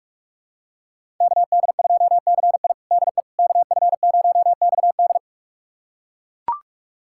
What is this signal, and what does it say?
Morse code, heard as a keyed tone, about 41 wpm, 700 Hz. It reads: KD2CI DE KF0XD
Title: USA Callsign DE Callsign - 5 Characters - 45wpm